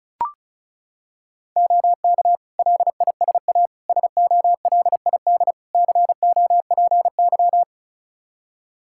Morse code, heard as a keyed tone, about 34 wpm, 700 Hz. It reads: OK LISA SOLID COPY
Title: Practice QSO - Exchange 3 - More Info - 35wpm